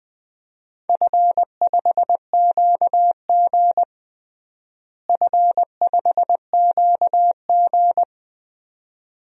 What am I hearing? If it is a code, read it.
Morse code, heard as a keyed tone, about 21 wpm, 700 Hz. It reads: F5QG F5QG
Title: DXCC-100 Call Signs - 4 Characters - Repeated 2x - 20wpm